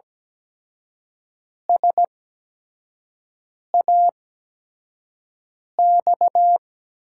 Morse code, heard as a keyed tone, about 17 wpm, spaced as for 5 wpm, 700 Hz. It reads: SAX
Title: USA Call Signs - 5 Characters - 8x Character Spacing - 17wpm